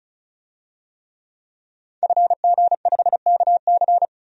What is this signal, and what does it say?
Morse code, heard as a keyed tone, about 34 wpm, 700 Hz. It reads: FG5KC
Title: DXCC-100 Call Signs - 5 Characters - 35wpm